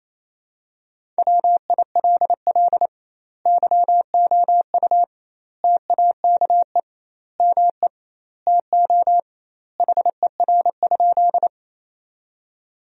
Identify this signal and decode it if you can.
Morse code, heard as a keyed tone, about 28 wpm, 700 Hz. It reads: WILL YOU TAKE ME TO HER?
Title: 2x Word-Spacing - Sentences from Top 200 Words - 28wpm